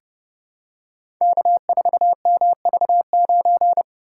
Morse code, heard as a keyed tone, about 29 wpm, 700 Hz. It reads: K4MV9
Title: Code Groups - 5 Letter:Numbers - 30wpm